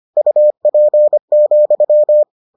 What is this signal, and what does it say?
Morse code, heard as a keyed tone, about 25 wpm, 600 Hz. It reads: UP,